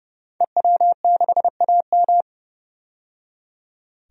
Morse code, heard as a keyed tone, about 30 wpm, 700 Hz. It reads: EW6AM
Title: DXCC-100 Call Signs - 5 Characters - 30wpm